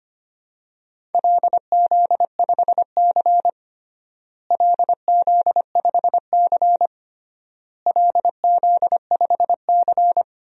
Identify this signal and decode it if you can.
Morse code, heard as a keyed tone, about 25 wpm, 700 Hz. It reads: LZ5C LZ5C LZ5C
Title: DXCC-100 Call Signs - 4 Characters - Repeated 3x - 25wpm